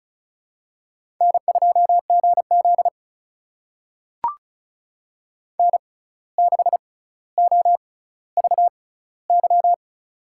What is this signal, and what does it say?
Morse code, heard as a keyed tone, about 33 wpm, spaced as for 16 wpm, 700 Hz. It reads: N2GZ N6OVY
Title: USA Call Signs - 6x Character Spacing - 35wpm